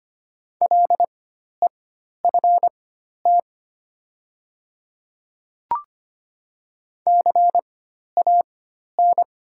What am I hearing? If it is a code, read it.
Morse code, heard as a keyed tone, about 25 wpm, spaced as for 12 wpm, 700 Hz. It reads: LEFT CAN
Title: 4x Character Spacing - Top 500 Words - 25wpm